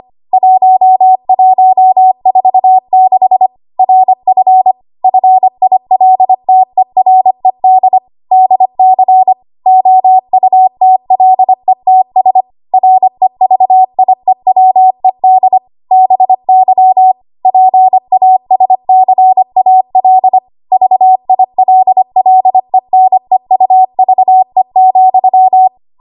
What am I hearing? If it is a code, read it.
Morse code, heard as a keyed tone, about 25 wpm, 750 Hz. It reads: 1146 RF FILTERED DC OUTLETS REVIEWED BY PASCAL VILLENEUVE,
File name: Morse / 25WPM